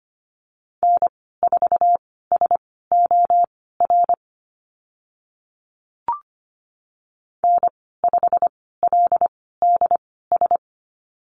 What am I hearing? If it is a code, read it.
Morse code, heard as a keyed tone, about 25 wpm, spaced as for 16 wpm, 700 Hz. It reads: N4SOR N5LDS
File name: Lesson 15g - Number 5 - Call Signs 25wpm